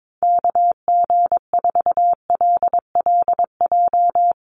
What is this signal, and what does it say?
Morse code, heard as a keyed tone, about 22 wpm, 700 Hz. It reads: KG4LLJ
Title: USA Call Signs 22wpm